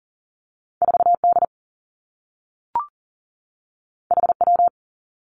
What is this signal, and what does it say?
Morse code, heard as a keyed tone, about 41 wpm, 700 Hz. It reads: 4 D H W
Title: Two Challenging Letter-Number Combinations 40wpm